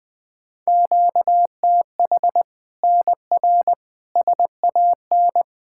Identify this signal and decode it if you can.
Morse code, heard as a keyed tone, about 20 wpm, 700 Hz. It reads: QTH NR SAN
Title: Practice QSO - Exchange 1 - Beginning - 20wpm